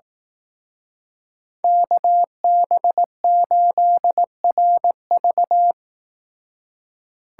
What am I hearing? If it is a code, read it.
Morse code, heard as a keyed tone, about 18 wpm, 700 Hz. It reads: KB8RV
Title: Speed Racing - USA 2x2 Call Signs 15wpm